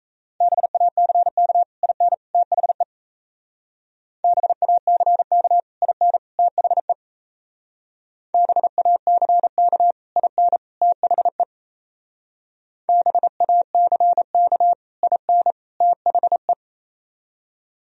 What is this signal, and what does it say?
Morse code, heard as a keyed tone, about 34 wpm, 700 Hz. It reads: BACK IN THE BACK IN THE BACK IN THE BACK IN THE
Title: Speed-Racing - Sets of 3 Words 28wpm